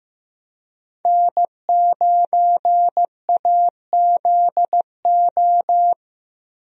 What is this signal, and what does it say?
Morse code, heard as a keyed tone, about 15 wpm, 700 Hz. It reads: N9AZO
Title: USA Call Signs - 5 Characters - Repeated 2x - 15wpm